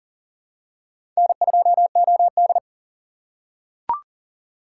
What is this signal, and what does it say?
Morse code, heard as a keyed tone, about 40 wpm, 700 Hz. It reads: N2OD
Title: USA Call Signs - 4 Characters - 4x Character Spacing - 40wpm